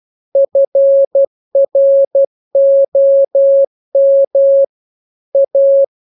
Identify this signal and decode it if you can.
Morse code, heard as a keyed tone, about 12 wpm, 550 Hz. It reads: FROM A
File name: jokes / joke 5 12 0000